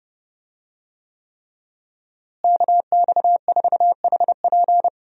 Morse code, heard as a keyed tone, about 30 wpm, 700 Hz. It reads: KX4HP
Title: USA Call Signs - 5 Characters - Repeated 3x - 30wpm